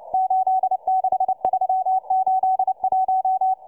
Morse code, heard as a keyed tone, about 29 wpm, 750 Hz, with radio noise. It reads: 86381